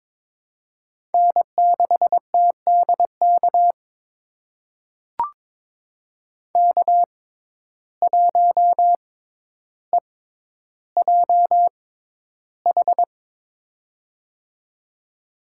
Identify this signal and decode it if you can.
Morse code, heard as a keyed tone, about 22 wpm, spaced as for 11 wpm, 700 Hz. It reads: N6TDK K1EJH
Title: USA Call Signs - 5 Characters - 6x Character Spacing - 22wpm